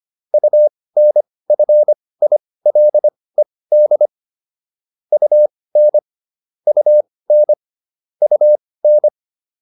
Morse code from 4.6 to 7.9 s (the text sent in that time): UN UN